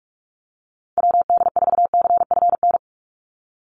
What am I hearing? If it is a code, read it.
Morse code, heard as a keyed tone, about 44 wpm, 700 Hz. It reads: WD4CFN
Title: NA SOTA Chaser Callsigns 45wpm